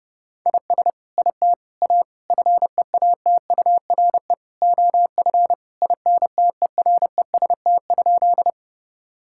Morse code, heard as a keyed tone, about 30 wpm, 700 Hz. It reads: IS IT A FEATURE OF INTEREST?
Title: Sentences from Top 1000 Words - Review Part 5 - 30wpm